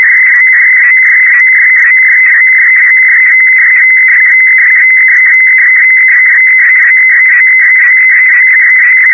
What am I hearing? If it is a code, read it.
Weather fax (radiofax).
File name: europe / d-6068-edzw testbild dcf54-1994-rb